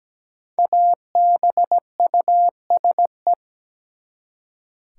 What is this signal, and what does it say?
Morse code, heard as a keyed tone, about 17 wpm, 700 Hz. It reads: ABUSE
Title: Five Letter Words - 17wpm